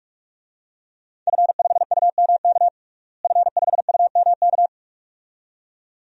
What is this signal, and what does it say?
Morse code, heard as a keyed tone, about 46 wpm, 700 Hz. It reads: F5UMK F5UMK
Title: DXCC-100 Call Signs - 5 Characters - Repeated 2x - 45wpm